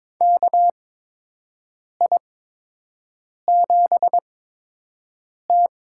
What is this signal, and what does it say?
Morse code, heard as a keyed tone, about 22 wpm, spaced as for 6 wpm, 700 Hz. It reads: KI7T
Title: USA Call Signs - 6 Characters - 8x Character Spacing - 22wpm